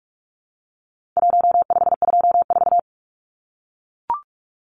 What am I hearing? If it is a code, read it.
Morse code, heard as a keyed tone, about 44 wpm, 700 Hz. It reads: 1524
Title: Numbers - 4-Digit 45wpm